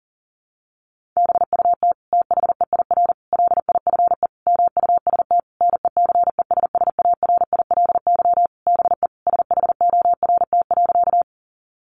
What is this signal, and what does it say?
Morse code, heard as a keyed tone, about 40 wpm, 700 Hz. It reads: BUT THEIR LIFE MUST NECESSARILY BE SHORT.
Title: Sentences from Top 1400 Words 40wpm